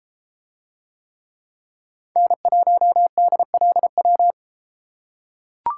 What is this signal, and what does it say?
Morse code, heard as a keyed tone, about 33 wpm, 700 Hz. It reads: N1DLW